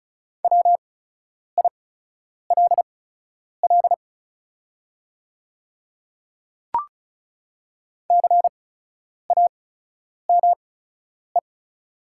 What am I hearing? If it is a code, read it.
Morse code, heard as a keyed tone, about 36 wpm, spaced as for 10 wpm, 700 Hz. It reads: WILL CAME
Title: 8x Character Spacing - Top 300 Words - 35wpm